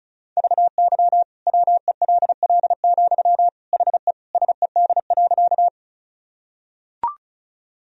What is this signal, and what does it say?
Morse code, heard as a keyed tone, about 36 wpm, 700 Hz. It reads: VY WELL, HE SED.
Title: Sentences from Top 200 Words and Abbreviations 35wpm